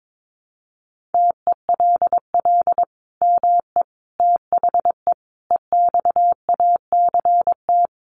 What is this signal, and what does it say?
Morse code, heard as a keyed tone, about 22 wpm, 700 Hz. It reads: TELL ME THE EXACT
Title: Sentences from Top 1300 Words 22wpm